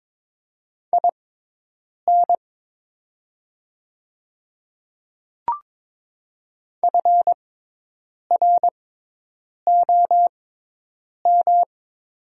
Morse code, heard as a keyed tone, about 23 wpm, spaced as for 8 wpm, 690 Hz. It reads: IN FROM
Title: 6x Character Spacing - Top 100 Words - 22wpm